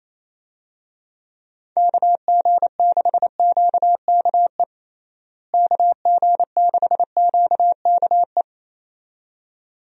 Morse code, heard as a keyed tone, about 28 wpm, 700 Hz. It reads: KG6QKE KG6QKE
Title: USA Call Signs - 6 Characters - Repeated 2x - 28wpm